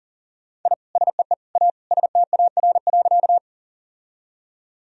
Morse code, heard as a keyed tone, about 37 wpm, 700 Hz. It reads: I SEE A STAR.
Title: Sentences from Top 1000 Words - Review Part 2 - 40wpm